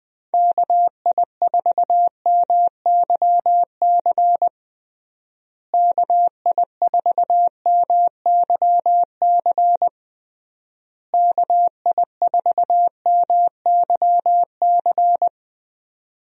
Morse code, heard as a keyed tone, about 20 wpm, 700 Hz. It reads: KI4MYC KI4MYC KI4MYC
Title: USA Call Signs - 6 Characters - Repeated 3x - 20wpm